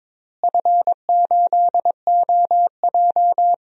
Morse code, heard as a keyed tone, about 22 wpm, 700 Hz. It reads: F8OJ
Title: DXCC-100 Call Signs - 4 Characters - Repeated 3x - 22wpm